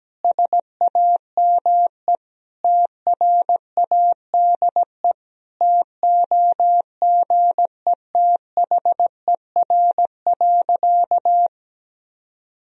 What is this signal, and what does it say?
Morse code, heard as a keyed tone, about 17 wpm, 700 Hz. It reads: SAME TRADE TOGETHER.
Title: Sentences from Top 1000 Words - Review Part 4 - 17wpm